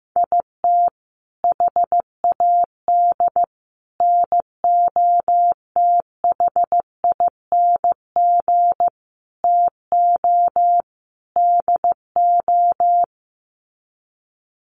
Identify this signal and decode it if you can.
Morse code, heard as a keyed tone, about 15 wpm, 700 Hz. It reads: IT HAD NOTHING TO DO
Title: Sets of 5 Words Encore2 15wpm